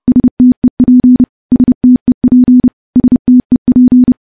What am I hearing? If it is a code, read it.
Morse code, heard as a keyed tone, about 31 wpm, 250 Hz. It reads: STEP STEP STEP
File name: morse / morse-training-30WPM